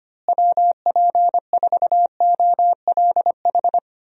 Morse code, heard as a keyed tone, about 25 wpm, 700 Hz. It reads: WP4OLH
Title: USA Call Signs - 6 Characters - Repeated 3x - 25wpm